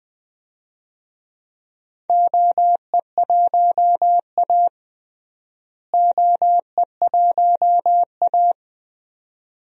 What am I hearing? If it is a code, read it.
Morse code, heard as a keyed tone, about 20 wpm, 700 Hz. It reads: OE1A OE1A
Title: DXCC-100 Call Signs - 4 Characters - Repeated 2x - 20wpm